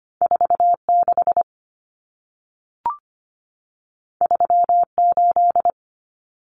Morse code, heard as a keyed tone, about 25 wpm, 700 Hz. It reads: 46 38
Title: Numbers - 2-Digit 25wpm